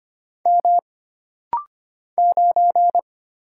Morse code, heard as a keyed tone, about 25 wpm, 700 Hz. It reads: M 9
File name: Single Character (Letter:Number:Punctuation:Prosign) - Rapid-Fire - 25wpm